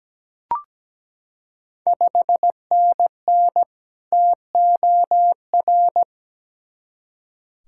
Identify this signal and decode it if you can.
Morse code, heard as a keyed tone, about 17 wpm, 700 Hz. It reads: 5NN TOR
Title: Ontario QSO Party Exchange 17wpm